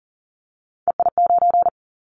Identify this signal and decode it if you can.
Morse code, heard as a keyed tone, about 39 wpm, 700 Hz. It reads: EI9
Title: Callsign Prefixes for Top 50 DXCC - 40wpm